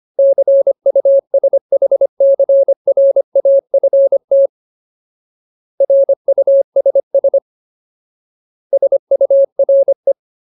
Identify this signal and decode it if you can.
Morse code, heard as a keyed tone, about 25 wpm, 550 Hz. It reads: CUSHCRAFT RUSS SURE
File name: Ultimate Word List / 25WPM Ultimate